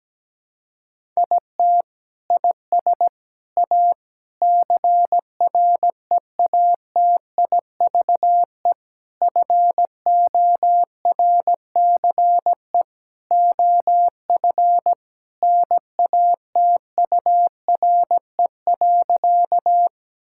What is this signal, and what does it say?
Morse code, heard as a keyed tone, about 17 wpm, 700 Hz. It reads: IT IS A CREATIVE FORCE OF NATURE.
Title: Sentences from Top 1000 Words - Review Part 5 - 17wpm